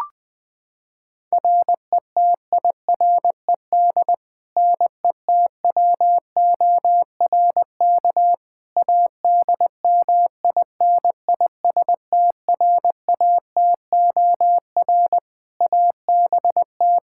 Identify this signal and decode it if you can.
Morse code, heard as a keyed tone, about 20 wpm, 700 Hz. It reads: RETIRED NETWORK ADMINISTRATOR ABT
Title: Practice QSO - Exchange 3 - More Info - 20wpm